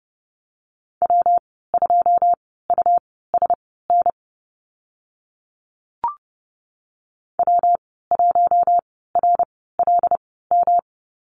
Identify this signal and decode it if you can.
Morse code, heard as a keyed tone, about 30 wpm, spaced as for 17 wpm, 700 Hz. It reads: W2USN W1RLM
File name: Lesson 19g - Letter w - Call Signs 30wpm